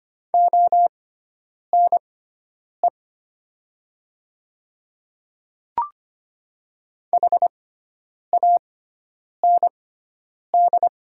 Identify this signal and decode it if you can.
Morse code, heard as a keyed tone, about 25 wpm, spaced as for 9 wpm, 700 Hz. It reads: ONE HAND
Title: 6x Character Spacing - Top 200 Words - 25wpm